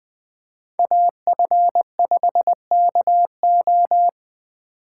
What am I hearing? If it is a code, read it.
Morse code, heard as a keyed tone, about 20 wpm, 700 Hz. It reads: AF5KO
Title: Speed Racing - USA 2x2 Call Signs 17wpm